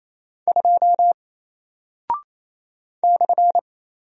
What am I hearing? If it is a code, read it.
Morse code, heard as a keyed tone, about 27 wpm, 700 Hz. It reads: 2 /
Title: Single Character (Letter:Number:Punctuation:Prosign) 28wpm